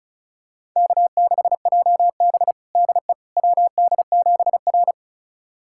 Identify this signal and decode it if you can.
Morse code, heard as a keyed tone, about 36 wpm, 700 Hz. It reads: K6JB DE WD7R
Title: USA Callsign DE Callsign - 4 Characters - 35wpm